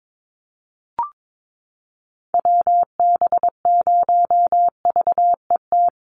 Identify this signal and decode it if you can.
Morse code, heard as a keyed tone, about 22 wpm, 700 Hz. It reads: WB0VET
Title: USA Call Signs 22wpm